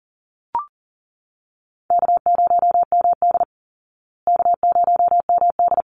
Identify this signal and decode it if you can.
Morse code, heard as a keyed tone, about 40 wpm, 700 Hz. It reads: K0MD K0MD
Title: Call Signs Repeated 3x - 40wpm